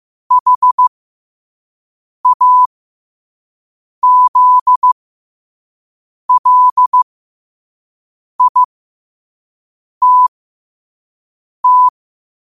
Morse code, heard as a keyed tone, about 15 wpm, spaced as for 5 wpm, 1000 Hz. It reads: HAZLITT
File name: qotdfiles / 5wpm